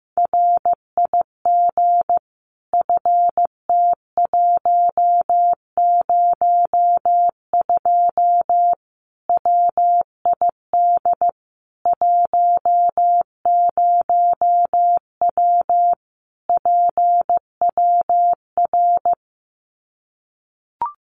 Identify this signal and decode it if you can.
Morse code, heard as a keyed tone, about 15 wpm, 700 Hz. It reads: RIG FT102 WID 10W PWR
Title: Common QSO Phrases 15wpm